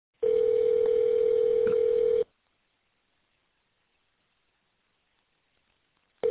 Ringback tone (North America), 440 + 480 Hz.